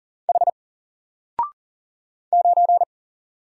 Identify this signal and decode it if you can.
Morse code, heard as a keyed tone, about 39 wpm, 700 Hz. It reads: H 9
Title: Single Letter-Number 40wpm